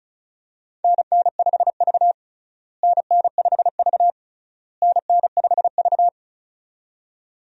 Morse code, heard as a keyed tone, about 36 wpm, 700 Hz. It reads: NN5V NN5V NN5V
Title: USA Call Signs - 4 Characters - Repeated 3x - 35wpm